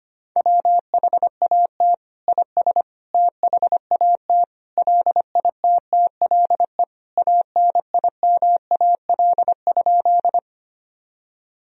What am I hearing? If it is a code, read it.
Morse code, heard as a keyed tone, about 25 wpm, 700 Hz. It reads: WHAT IS THAT LITTLE ANIMAL?
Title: Sentences from Top 1000 Words - Review Part 1 - 25wpm